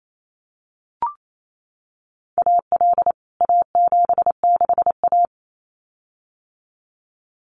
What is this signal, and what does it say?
Morse code, heard as a keyed tone, about 28 wpm, 700 Hz. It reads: AL A76A
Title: CWT Exchange 28wpm